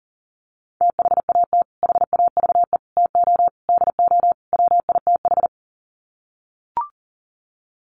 Morse code, heard as a keyed tone, about 40 wpm, 700 Hz. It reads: THAT HAVE TO DO WITH
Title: Sets of 5 Words Encore2 40wpm